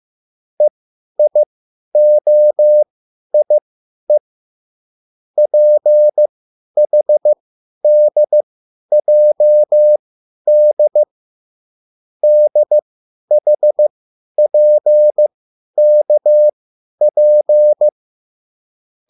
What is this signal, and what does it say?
Morse code, heard as a keyed tone, about 15 wpm, spaced as for 10 wpm, 600 Hz. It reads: EIOIE PHDJD DHPKP
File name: QSL / CW lesson